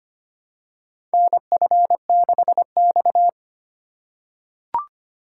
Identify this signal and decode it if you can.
Morse code, heard as a keyed tone, about 25 wpm, 700 Hz. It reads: NF6X
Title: USA Call Signs - 4 Characters - 4x Character Spacing - 25wpm